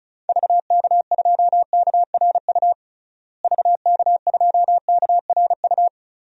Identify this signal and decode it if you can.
Morse code, heard as a keyed tone, about 36 wpm, 700 Hz. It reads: VK2KRU VK2KRU
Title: DXCC-100 Call Signs - 6 Characters - Repeated 2x - 35wpm